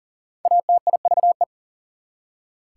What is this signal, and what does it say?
Morse code, heard as a keyed tone, about 40 wpm, 700 Hz. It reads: ATIVE
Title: Prefixes and Suffixes 40wpm